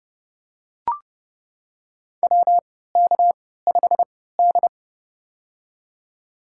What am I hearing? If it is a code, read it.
Morse code, heard as a keyed tone, about 30 wpm, spaced as for 17 wpm, 700 Hz. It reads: WK5D
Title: USA Call Signs - 4 Characters - 3x Character Spacing - 30wpm